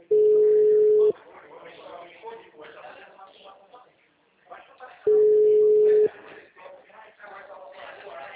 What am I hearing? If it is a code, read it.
Ringback tone (Europe, 425 Hz), with speech.
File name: out- -2014- - - .71234